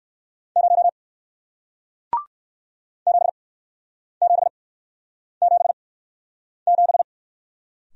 Morse code, heard as a keyed tone, about 59 wpm, spaced as for 9 wpm, 700 Hz. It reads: 0 7777